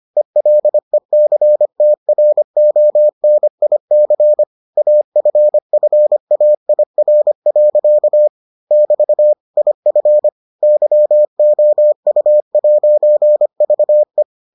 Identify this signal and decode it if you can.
Morse code, heard as a keyed tone, about 25 wpm, 600 Hz. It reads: ELECTRONIC AFFAIR. <BT> IF YOU'VE